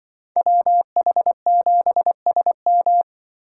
Morse code, heard as a keyed tone, about 25 wpm, 700 Hz. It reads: WH7SM
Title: Speed Racing - USA 2x2 Call Signs 20wpm